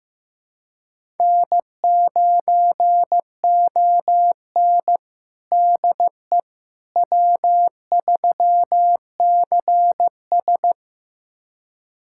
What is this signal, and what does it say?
Morse code, heard as a keyed tone, about 15 wpm, 700 Hz. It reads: N9ON DE W3CS
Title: USA Callsign DE Callsign - 4 Characters - 15wpm